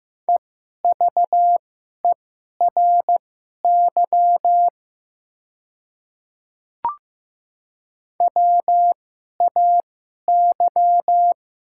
Morse code, heard as a keyed tone, about 15 wpm, spaced as for 11 wpm, 700 Hz. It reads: EVERY WAY
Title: 2x Character Spacing - Top 200 Words - 15wpm